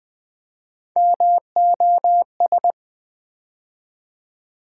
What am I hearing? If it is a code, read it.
Morse code, heard as a keyed tone, about 20 wpm, 700 Hz. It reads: MOS
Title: Build A Word - 3 Letters At A Time - 8-Letter Words - 20wpm